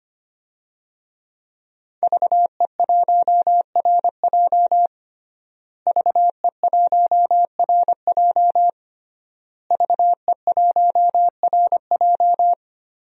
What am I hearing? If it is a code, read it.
Morse code, heard as a keyed tone, about 25 wpm, 700 Hz. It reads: VE1RJ VE1RJ VE1RJ
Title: DXCC-100 Call Signs - 5 Characters - Repeated 3x - 25wpm